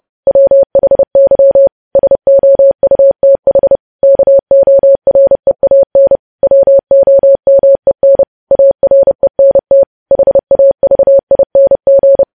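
Morse code, heard as a keyed tone, about 30 wpm, 550 Hz. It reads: WHY SOUTH KOREAN WOMEN ARENT HAVING
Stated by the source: News headlines sent in Morse code at 30.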